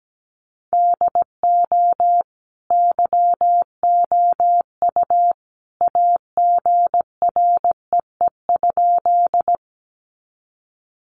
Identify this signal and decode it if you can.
Morse code, heard as a keyed tone, about 17 wpm, 700 Hz. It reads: DO YOU AGREE?
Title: Sentences from Top 1100 Words 17wpm